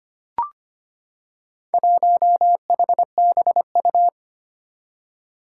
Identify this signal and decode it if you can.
Morse code, heard as a keyed tone, about 25 wpm, 700 Hz. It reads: 1HBU
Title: Four Challenging Letter-Number Combinations 25wpm